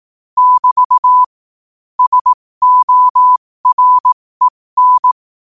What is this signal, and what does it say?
Morse code, heard as a keyed tone, about 18 wpm, spaced as for 15 wpm, 1000 Hz. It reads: <BT> SOREN
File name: qotdfiles / 15wpm